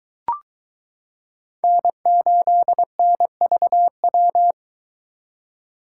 Morse code, heard as a keyed tone, about 23 wpm, 700 Hz. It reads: N8NVW